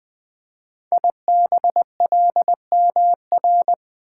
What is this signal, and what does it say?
Morse code, heard as a keyed tone, about 20 wpm, 700 Hz. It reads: IBLMR
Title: Code Groups - 5 Letters - 20wpm